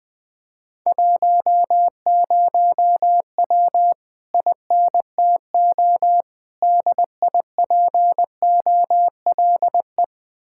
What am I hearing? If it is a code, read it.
Morse code, heard as a keyed tone, about 20 wpm, 700 Hz. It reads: 10W INTO DIPOLE
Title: Speed-Racing Common QSO Phrases - 17wpm